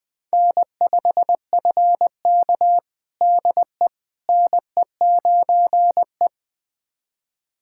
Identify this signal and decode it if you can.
Morse code, heard as a keyed tone, about 20 wpm, 700 Hz. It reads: N5FK DE NE9E
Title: USA Callsign DE Callsign - 4 Characters - 20wpm